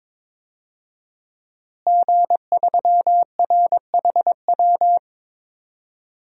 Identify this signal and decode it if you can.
Morse code, heard as a keyed tone, about 22 wpm, 700 Hz. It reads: G3RHW
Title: DXCC-100 Call Signs - 5 Characters - Repeated 3x - 22wpm